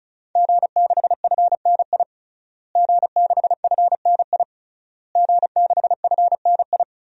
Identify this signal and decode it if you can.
Morse code, heard as a keyed tone, about 34 wpm, 700 Hz. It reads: G6FNI G6FNI G6FNI
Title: DXCC-100 Call Signs - 5 Characters - Repeated 3x - 35wpm